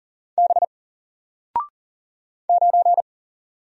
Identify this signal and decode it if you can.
Morse code, heard as a keyed tone, about 40 wpm, 700 Hz. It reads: B 9
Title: Single Letter-Number Alt 40wpm